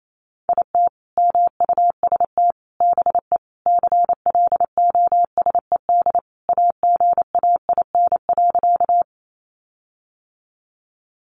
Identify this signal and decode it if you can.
Morse code, heard as a keyed tone, about 28 wpm, 700 Hz. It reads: IT MUST BE CLOSED AGAIN.
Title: Sentences from Top 1400 Words 28wpm